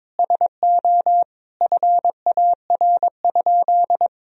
Morse code, heard as a keyed tone, about 22 wpm, 700 Hz. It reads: SO FAR?
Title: Sentences from Top 300 Words and Abbreviations 22wpm